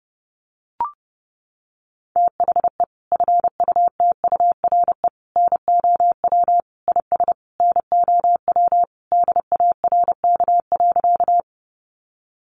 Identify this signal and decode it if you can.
Morse code, heard as a keyed tone, about 30 wpm, 700 Hz. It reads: THE FUTURE NOW IS NOW DARK.
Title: Sentences from Top 700 Words 30wpm